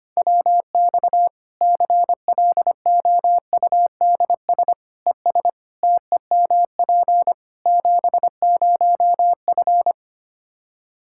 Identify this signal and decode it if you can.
Morse code, heard as a keyed tone, about 25 wpm, 700 Hz. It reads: WX CLOUDS ES TEMP 70F
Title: Practice QSO - Exchange 2 - Info - 25wpm